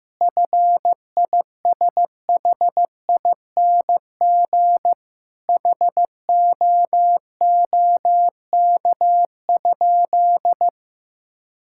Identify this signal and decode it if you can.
Morse code, heard as a keyed tone, about 16 wpm, 700 Hz. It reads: FISHING HOOK?
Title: Sentences from Top 1600 Words 15wpm